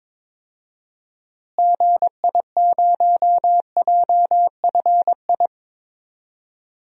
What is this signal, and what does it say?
Morse code, heard as a keyed tone, about 22 wpm, 700 Hz. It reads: GI0JFI
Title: DXCC-100 Call Signs - 6 Characters - Repeated 3x - 22wpm